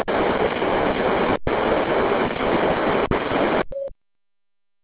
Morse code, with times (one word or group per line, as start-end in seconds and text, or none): none